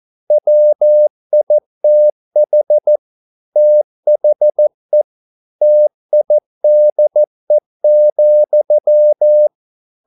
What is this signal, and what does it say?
Morse code, heard as a keyed tone, about 14 wpm, 600 Hz. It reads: WITH THE TIDE,
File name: mp3 files / a dogs mistake - 14wpm